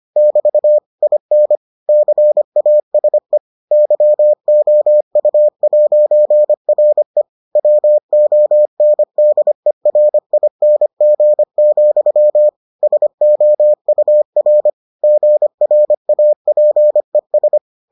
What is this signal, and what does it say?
Morse code, heard as a keyed tone, about 25 wpm, 600 Hz. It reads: <BT> IN CASE YOU'RE WONDERING, SOUR GRAPES